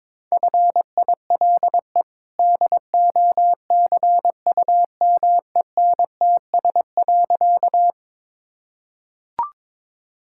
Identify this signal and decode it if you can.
Morse code, heard as a keyed tone, about 22 wpm, 700 Hz. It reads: FILE DOCUMENTS.
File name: practice / 22wpm / Easy Sentences with Hints - 22wpm